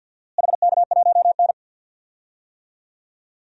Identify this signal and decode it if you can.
Morse code, heard as a keyed tone, about 51 wpm, 700 Hz. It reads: HK1N
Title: DXCC-100 Call Signs - 4 Characters - Repeated 2x - 50wpm